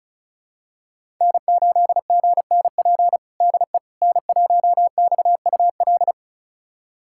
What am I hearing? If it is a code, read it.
Morse code, heard as a keyed tone, about 36 wpm, 700 Hz. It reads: N8GNP DE N1XUL
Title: USA Callsign DE Callsign - 5 Characters - 35wpm